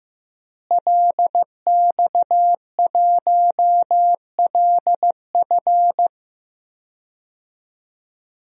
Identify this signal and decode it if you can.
Morse code, heard as a keyed tone, about 15 wpm, 700 Hz. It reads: LX1LF
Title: DXCC-100 Call Signs - 5 Characters - Repeated 3x - 15wpm